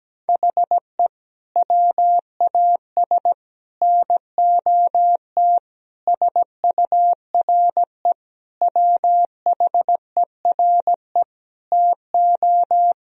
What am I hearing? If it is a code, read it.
Morse code, heard as a keyed tone, about 17 wpm, 700 Hz. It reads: HE WAS NOT SURE WHERE TO
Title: Sentences from Top 1000 Words - Review Part 1 - 17wpm